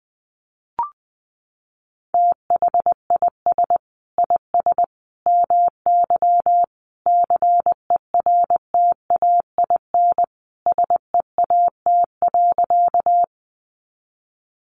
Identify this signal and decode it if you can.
Morse code, heard as a keyed tone, about 20 wpm, 700 Hz. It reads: THIS IS MY CERTAIN SEAT.
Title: Sentences from Top 1500 Words 20wpm